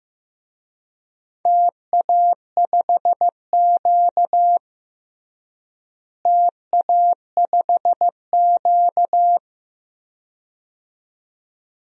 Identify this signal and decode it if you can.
Morse code, heard as a keyed tone, about 15 wpm, 700 Hz. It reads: TA5Q TA5Q
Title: DXCC-100 Call Signs - 4 Characters - Repeated 2x - 15wpm